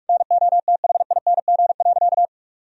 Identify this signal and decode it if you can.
Morse code, heard as a keyed tone, about 46 wpm, 700 Hz. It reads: NOTHING.
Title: Sentences from Top 1000 Words - Review Part 3 - 45wpm